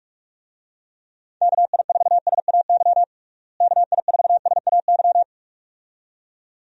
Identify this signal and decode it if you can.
Morse code, heard as a keyed tone, about 41 wpm, 700 Hz. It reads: KI4SAY KI4SAY
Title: USA Call Signs - 6 Characters - Repeated 2x - 45wpm